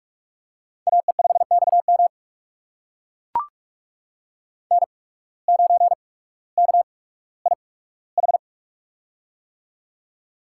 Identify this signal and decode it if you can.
Morse code, heard as a keyed tone, about 44 wpm, spaced as for 19 wpm, 700 Hz. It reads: AE5XM N9KIH
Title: USA Call Signs - 5 Characters - 8x Character Spacing - 45wpm